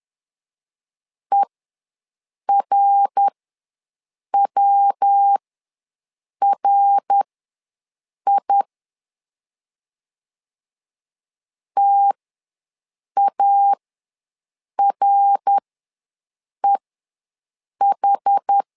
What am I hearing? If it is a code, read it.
Morse code, heard as a keyed tone, about 11 wpm, spaced as for 6 wpm, 800 Hz. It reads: ERWRI TAREH
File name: LEZIONI / 3 MWRO / 3 MWR ws 10wpm 2151